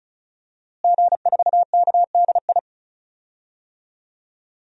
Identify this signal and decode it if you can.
Morse code, heard as a keyed tone, about 34 wpm, 700 Hz. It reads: G4KDI
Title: DXCC-100 Call Signs - 5 Characters - Repeated 3x - 35wpm